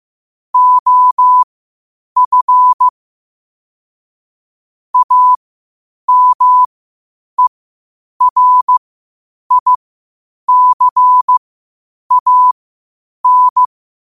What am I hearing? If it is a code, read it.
Morse code, heard as a keyed tone, about 15 wpm, spaced as for 8 wpm, 1000 Hz. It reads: OF AMERICAN
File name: qotdfiles / 8wpm